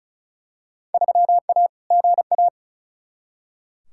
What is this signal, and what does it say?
Morse code, heard as a keyed tone, about 36 wpm, 700 Hz. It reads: 3A GA
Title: Field Day Exchanges - 35wpm